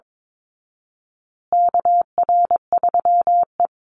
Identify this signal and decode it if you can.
Morse code, heard as a keyed tone, about 22 wpm, 700 Hz. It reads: KR3E
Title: Call Signs Repeated 2x - 22wpm